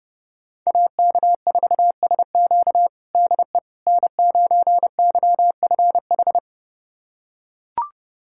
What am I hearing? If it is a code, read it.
Morse code, heard as a keyed tone, about 30 wpm, 700 Hz. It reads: AK4SQ DE N9YFH
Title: USA Callsign DE Callsign - 5 Characters - 30wpm